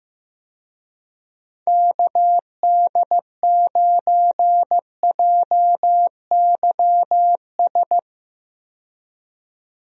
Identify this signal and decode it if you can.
Morse code, heard as a keyed tone, about 15 wpm, 700 Hz. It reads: KD9JYS
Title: USA Call Signs - 6 Characters - 15wpm